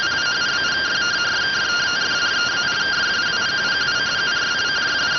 Two-tone teleprinter signal, about 170 Hz shift.